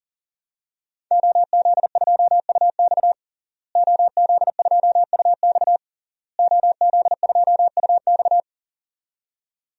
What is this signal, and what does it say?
Morse code, heard as a keyed tone, about 40 wpm, 700 Hz. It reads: OZ2UX OZ2UX OZ2UX
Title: DXCC-100 Call Signs - 5 Characters - Repeated 3x - 40wpm